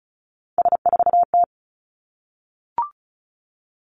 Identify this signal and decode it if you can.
Morse code, heard as a keyed tone, about 34 wpm, 700 Hz. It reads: S4T
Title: Three Challenging Letter-Number Combinations 35wpm